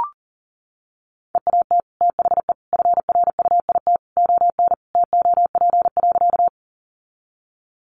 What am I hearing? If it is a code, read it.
Morse code, heard as a keyed tone, about 40 wpm, 700 Hz. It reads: EAT THE FRUIT ON TOP.
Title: Sentences from Top 1300 Words 40wpm